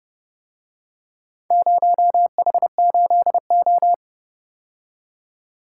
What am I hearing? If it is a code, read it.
Morse code, heard as a keyed tone, about 29 wpm, 700 Hz. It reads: 0H8O